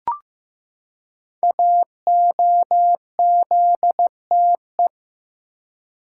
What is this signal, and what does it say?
Morse code, heard as a keyed tone, about 15 wpm, 700 Hz. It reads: AOZTE
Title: Code Groups - 5 Letters - 15wpm